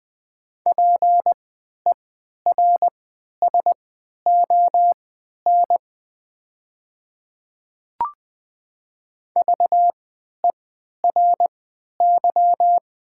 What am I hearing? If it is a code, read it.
Morse code, heard as a keyed tone, about 20 wpm, spaced as for 11 wpm, 700 Hz. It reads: PERSON VERY
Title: 3x Character Spacing - Top 500 Words - 20wpm